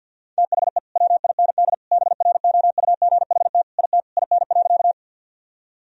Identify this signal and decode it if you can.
Morse code, heard as a keyed tone, about 50 wpm, 700 Hz. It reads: THE WIND BROUGHT IT IN.
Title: Sentences from Top 1000 Words - Review Part 2 - 50wpm